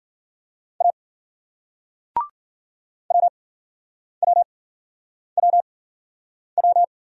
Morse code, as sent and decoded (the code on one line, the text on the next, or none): .- / .-- .-- .-- .--
A WWWW